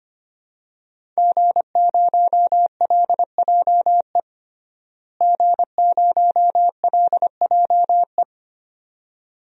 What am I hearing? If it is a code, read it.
Morse code, heard as a keyed tone, about 25 wpm, 700 Hz. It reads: G0LJE G0LJE
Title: DXCC-100 Call Signs - 5 Characters - Repeated 2x - 25wpm